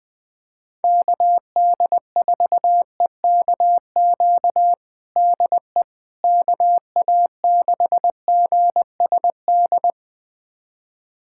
Morse code, heard as a keyed tone, about 20 wpm, 700 Hz. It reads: KD4EKQ DE KA6GSD
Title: USA Callsign DE Callsign - 6 Characters - 20wpm